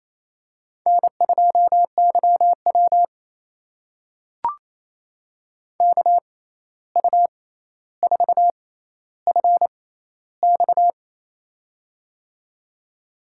Morse code, heard as a keyed tone, about 28 wpm, spaced as for 13 wpm, 700 Hz. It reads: N2YW KU4FX
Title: USA Call Signs - 6x Character Spacing - 28wpm